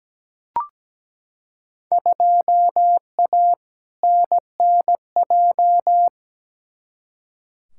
Morse code, heard as a keyed tone, about 17 wpm, 700 Hz. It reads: 2A NNJ
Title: Field Day Exchanges - 17wpm